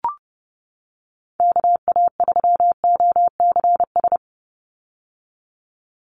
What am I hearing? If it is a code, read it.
Morse code, heard as a keyed tone, about 30 wpm, 700 Hz. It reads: KA3OCS
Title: Call Signs - 30wpm